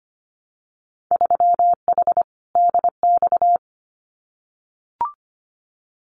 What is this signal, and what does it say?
Morse code, heard as a keyed tone, about 25 wpm, 700 Hz. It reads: 3H DX
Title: Winter Field Day Exchanges Encore 25wpm